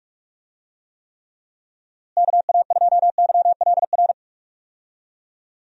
Morse code, heard as a keyed tone, about 46 wpm, 700 Hz. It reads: KA2YLR
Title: USA Call Signs - 6 Characters - Repeated 3x - 45wpm